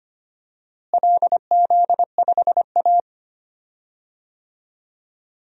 Morse code, heard as a keyed tone, about 25 wpm, 700 Hz. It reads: LZ5A
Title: DXCC-100 Call Signs - 4 Characters - Repeated 3x - 25wpm